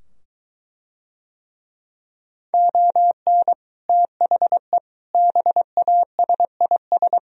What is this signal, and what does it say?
Morse code, heard as a keyed tone, about 23 wpm, 700 Hz. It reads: ON THE BASIS